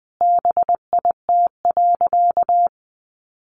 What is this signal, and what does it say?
Morse code, heard as a keyed tone, about 20 wpm, 700 Hz. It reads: BIT.
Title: Sentences from Top 900 Words 20wpm